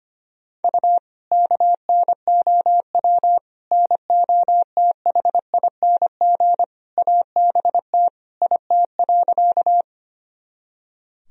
Morse code, heard as a keyed tone, about 25 wpm, 700 Hz. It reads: U KNOW NOTHING ABT IT.
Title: Sentences from Top 500 Words and Abbreviations 25wpm